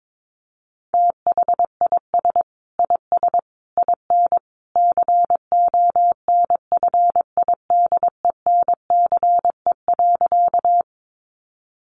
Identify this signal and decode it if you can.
Morse code, heard as a keyed tone, about 22 wpm, 700 Hz. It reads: THIS IS IN CONFIDENCE.
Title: Sentences from Top 1500 Words 22wpm